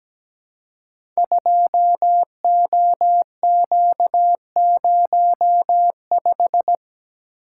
Morse code, heard as a keyed tone, about 17 wpm, 700 Hz. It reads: 2OQ05
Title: Code Groups - 5 Letter:Numbers - 17wpm